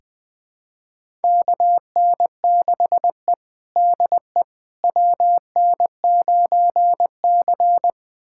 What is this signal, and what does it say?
Morse code, heard as a keyed tone, about 20 wpm, 700 Hz. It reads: KN6E DE WN9C
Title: USA Callsign DE Callsign - 4 Characters - 20wpm